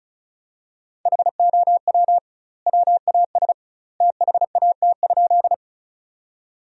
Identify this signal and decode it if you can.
Morse code, heard as a keyed tone, about 34 wpm, 700 Hz. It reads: HOW WAS THAT?
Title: 2x Word-Spacing - Sentences from Top 100 Words - 35wpm